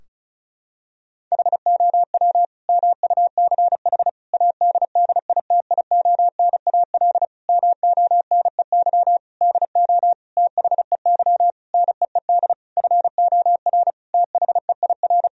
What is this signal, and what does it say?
Morse code, heard as a keyed tone, about 34 wpm, 700 Hz. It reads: HOW MUCH ADDITIONAL MONEY DO THEY NEED FOR THEIR
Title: Sentences from Top 1000 Words - Review Part 4 - 35wpm